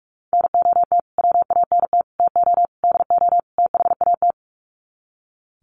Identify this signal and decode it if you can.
Morse code, heard as a keyed tone, about 44 wpm, 700 Hz. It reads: NOT WANT TO DO THAT
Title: Sets of 5 Words Encore 45wpm